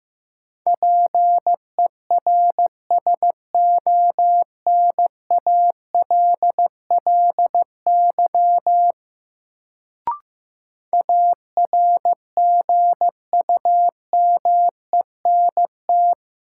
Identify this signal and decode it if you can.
Morse code, heard as a keyed tone, about 15 wpm, 700 Hz. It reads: PERSONALLY ARGUMENT
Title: Sentences from Top 1400 Words - New Word Drill Alt - 15wpm